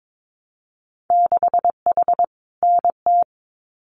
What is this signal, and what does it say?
Morse code, heard as a keyed tone, about 22 wpm, 700 Hz. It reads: 6H NT
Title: Winter Field Day Exchanges Encore 22wpm